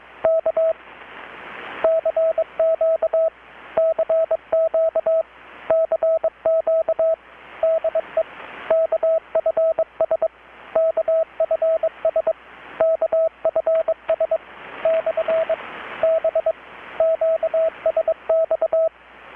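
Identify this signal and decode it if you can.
Morse code, heard as a keyed tone, about 22 wpm, 650 Hz, with radio noise. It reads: K CQ CQ CQ DE KFS KFS KFS / B QSX
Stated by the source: Morse from KPH's Night of Nights XX maritime radiotelegraphy event, received in Chiapas, Mexico about 3,800 km away on a Kenwood R-600 with a grounded 10 m long-wire antenna.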